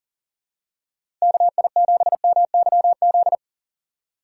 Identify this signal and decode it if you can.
Morse code, heard as a keyed tone, about 40 wpm, 700 Hz. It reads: KI7MYZ
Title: USA Call Signs - 6 Characters - 4x Character Spacing - 40wpm